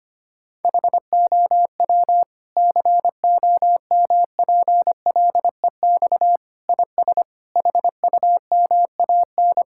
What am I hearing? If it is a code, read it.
Morse code, heard as a keyed tone, about 25 wpm, 700 Hz. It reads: HOW COMPLEX IS HUMAN
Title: Sentences from Top 1000 Words - Review Part 6 - 25wpm